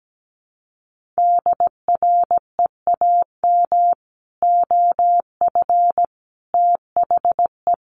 Morse code, heard as a keyed tone, about 17 wpm, 700 Hz. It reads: DREAM OF THE
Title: Sentences from Top 1500 Words 17wpm